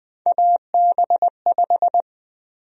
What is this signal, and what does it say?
Morse code, heard as a keyed tone, about 20 wpm, 700 Hz. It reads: AB5
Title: USA Call Sign Prefixes - 20wpm